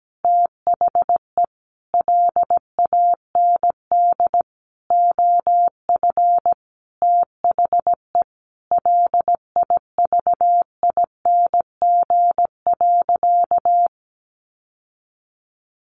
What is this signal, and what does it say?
Morse code, heard as a keyed tone, about 17 wpm, 700 Hz. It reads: THE LAND OF THE LIVING.
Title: Sentences from Top 1100 Words 17wpm